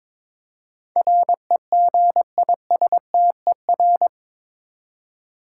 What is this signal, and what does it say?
Morse code, heard as a keyed tone, about 22 wpm, 700 Hz. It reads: REGISTER
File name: practice / 22wpm / Eight Letter Words - 22wpm